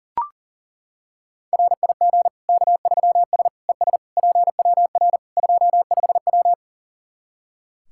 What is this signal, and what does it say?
Morse code, heard as a keyed tone, about 40 wpm, 700 Hz. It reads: RIG K3S ES PWR 25W
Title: Practice QSO - Exchange 2 - Info - 40wpm